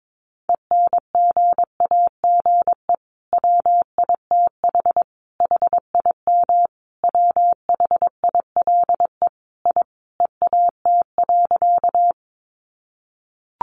Morse code, heard as a keyed tone, about 22 wpm, 700 Hz. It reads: ENGAGE WITH HIM WHILE I EAT.
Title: Sentences from Top 1500 Words 22wpm